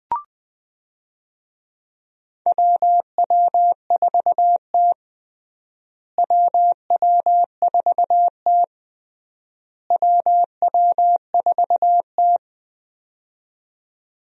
Morse code, heard as a keyed tone, about 19 wpm, 700 Hz. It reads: WW4T WW4T WW4T
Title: USA Call Signs - 4 Characters - Repeated 3x - 20wpm